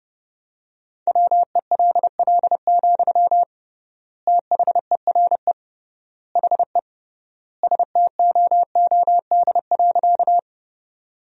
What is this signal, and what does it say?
Morse code, heard as a keyed tone, about 32 wpm, 700 Hz. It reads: WELL, THERE HE STOOD.
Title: 3x Word-Spacing - Sentences from Top 500 Words - 30wpm